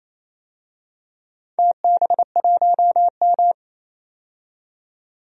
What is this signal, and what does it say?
Morse code, heard as a keyed tone, about 28 wpm, 700 Hz. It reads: TB1M
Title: DXCC-100 Call Signs - 4 Characters - 28wpm